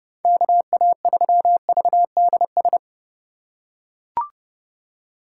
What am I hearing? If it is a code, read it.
Morse code, heard as a keyed tone, about 30 wpm, 700 Hz. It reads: KA3VDS
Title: USA Call Signs - 6 Characters - 3x Character Spacing - 30wpm